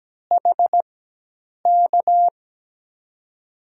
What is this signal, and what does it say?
Morse code, heard as a keyed tone, about 17 wpm, spaced as for 8 wpm, 700 Hz. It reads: HK
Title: USA Call Signs - 6 Characters - 4x Character Spacing - 17wpm